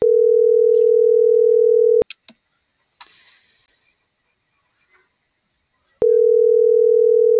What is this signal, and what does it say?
Ringback tone (North America), 440 + 480 Hz.